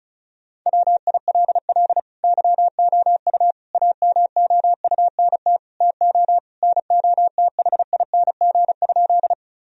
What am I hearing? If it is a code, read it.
Morse code, heard as a keyed tone, about 36 wpm, 700 Hz. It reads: WILL YOU AMOUNT TO NOTHING?
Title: Sentences from Top 1000 Words - Review Part 4 - 35wpm